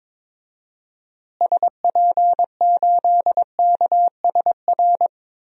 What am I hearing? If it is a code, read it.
Morse code, heard as a keyed tone, about 22 wpm, 700 Hz. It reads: SP8KSR
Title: DXCC-100 Call Signs - 6 Characters - Repeated 3x - 22wpm